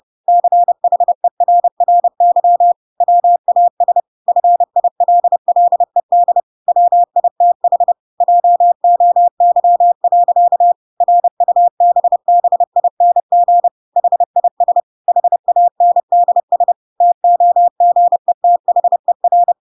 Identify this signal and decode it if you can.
Morse code, heard as a keyed tone, about 30 wpm, 700 Hz. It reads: CHERRY WAS FILLED WITH JOY. RUBBING HIS HANDS TOGETHER
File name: Pinocchio / Pinocchio ch1 30 30 ik1hge